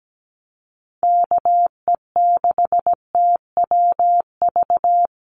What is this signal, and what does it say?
Morse code, heard as a keyed tone, about 17 wpm, 700 Hz. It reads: KE6TWV
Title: USA Call Signs Repeated 2x 17wpm